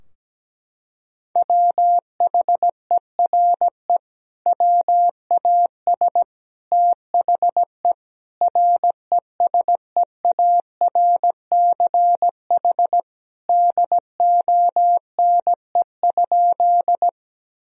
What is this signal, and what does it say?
Morse code, heard as a keyed tone, about 17 wpm, 700 Hz. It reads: WHERE WAS THE RESEARCH DONE?
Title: Sentences from Top 1000 Words - Review Part 4 - 17wpm